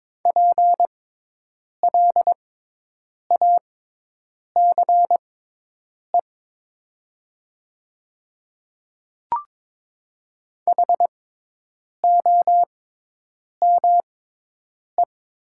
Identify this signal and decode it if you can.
Morse code, heard as a keyed tone, about 22 wpm, spaced as for 8 wpm, 700 Hz. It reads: PLACE HOME
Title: 6x Character Spacing - Top 200 Words - 22wpm